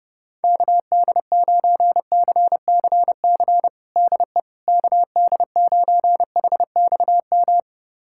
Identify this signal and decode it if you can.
Morse code, heard as a keyed tone, about 30 wpm, 700 Hz. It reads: KD9CCC DE KD9HXM
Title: USA Callsign DE Callsign - 6 Characters - 30wpm